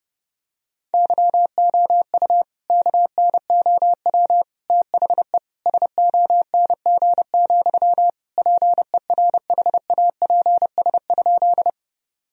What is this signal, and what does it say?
Morse code, heard as a keyed tone, about 30 wpm, 700 Hz. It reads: YOU KNOW THE SONG, PERHAPS?
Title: Sentences from Top 1000 Words - Review Part 2 - 30wpm